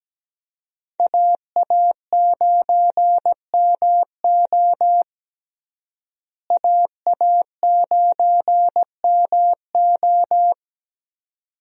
Morse code, heard as a keyed tone, about 17 wpm, 700 Hz. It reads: AA9MO AA9MO
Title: USA Call Signs - 5 Characters - Repeated 2x - 17wpm